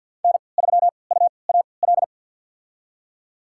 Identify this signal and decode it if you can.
Morse code, heard as a keyed tone, about 47 wpm, spaced as for 28 wpm, 700 Hz. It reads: N3UAL
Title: USA Call Signs - 5 Characters - 3x Character Spacing - 50wpm